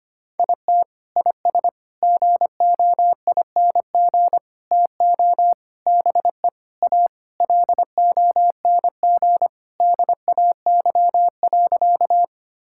Morse code, heard as a keyed tone, about 25 wpm, 700 Hz. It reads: IT IS GOING TO BE A LONG DAY.
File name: practice / 25wpm / Sentences with 4-Word Phrases - 25wpm